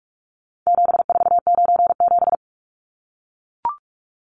Morse code, heard as a keyed tone, about 44 wpm, 700 Hz. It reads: 7497
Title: Numbers - 4-Digit 45wpm